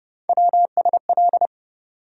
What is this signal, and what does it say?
Morse code, heard as a keyed tone, about 31 wpm, 700 Hz. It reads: WSL
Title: USA Call Sign Postfixes - 3 Letters 30wpm